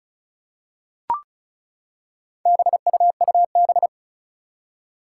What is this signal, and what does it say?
Morse code, heard as a keyed tone, about 36 wpm, 700 Hz. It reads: BUUB
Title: Four Challenging Letter-Number Combinations 35wpm